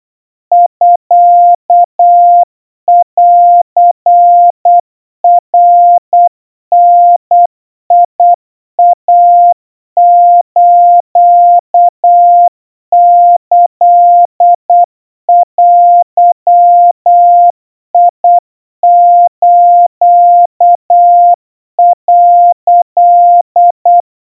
Morse code (Japanese Wabun code), heard as a keyed tone, about 8 wpm, 700 Hz. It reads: ミンナダイスキデス」
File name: kawa / Morse sample / kamyu 30 cpm 700 Hz Crispy